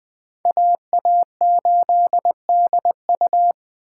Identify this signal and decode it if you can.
Morse code, heard as a keyed tone, about 20 wpm, 700 Hz. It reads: AA8DU
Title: Speed Racing - USA 2x2 Call Signs 17wpm